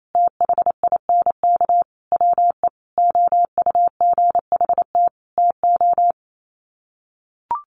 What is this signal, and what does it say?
Morse code, heard as a keyed tone, about 28 wpm, 700 Hz. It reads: THINK WE OUGHT TO
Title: Sets of 5 Words Encore 28wpm